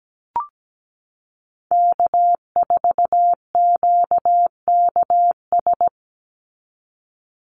Morse code, heard as a keyed tone, about 17 wpm, 700 Hz. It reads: K4QKS
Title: USA Call Signs 17wpm